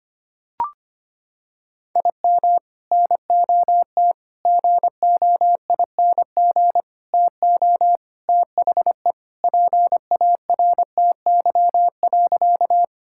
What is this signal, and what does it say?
Morse code, heard as a keyed tone, about 25 wpm, 700 Hz. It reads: IM NOT GOING TO THE PARTY.
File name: practice / 25wpm / Sentences with 3-Word Phrases - 25wpm